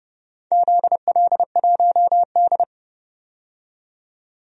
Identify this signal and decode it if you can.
Morse code, heard as a keyed tone, about 30 wpm, 700 Hz. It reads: ZL1D
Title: DXCC-100 Call Signs - 4 Characters - 30wpm